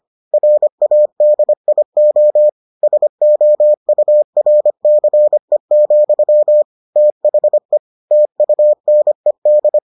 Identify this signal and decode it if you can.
Morse code, heard as a keyed tone, about 25 wpm, 600 Hz. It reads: RADIO SOURCE, THE TUNED